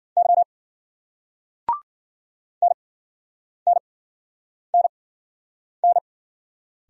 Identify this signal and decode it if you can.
Morse code, heard as a keyed tone, about 45 wpm, spaced as for 9 wpm, 700 Hz. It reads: Y NNNN
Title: Speed-Racing Single Letter-Number 40wpm